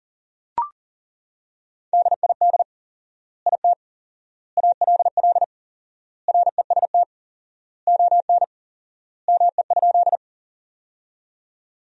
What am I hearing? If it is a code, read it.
Morse code, heard as a keyed tone, about 40 wpm, 700 Hz. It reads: DID IT ALL REST ON ME?
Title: 4x Word-Spacing - Sentences from Top 500 Words - 40wpm